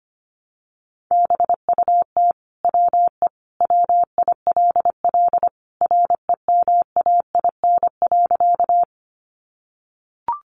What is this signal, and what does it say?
Morse code, heard as a keyed tone, about 25 wpm, 700 Hz. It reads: BUT WE WILL REMAIN.
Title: Sentences from Top 1200 Words 25wpm